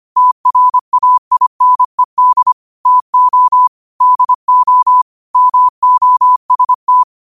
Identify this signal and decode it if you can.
Morse code, heard as a keyed tone, about 25 wpm, 1000 Hz. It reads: TRAINED TO DO MOST
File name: qotdfiles / 25wpm